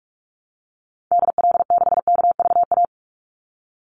Morse code, heard as a keyed tone, about 44 wpm, 700 Hz. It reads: DL6KVA
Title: Call Signs Repeated 2x - 45wpm